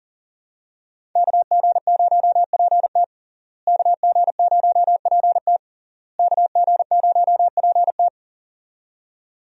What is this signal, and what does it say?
Morse code, heard as a keyed tone, about 40 wpm, 700 Hz. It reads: KG0PT KG0PT KG0PT
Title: USA Call Signs - 5 Characters - Repeated 3x - 40wpm